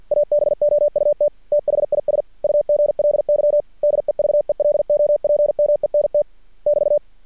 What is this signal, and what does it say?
Morse code, heard as a keyed tone, about 50 wpm, 590 Hz. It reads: ABOUT THIS UGLY DEVELOPMENT <BT>
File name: archived / CRSnet